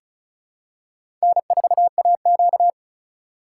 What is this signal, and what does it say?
Morse code, heard as a keyed tone, about 36 wpm, 700 Hz. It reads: N4AQ
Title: USA Call Signs - 4 Characters - 3x Character Spacing - 35wpm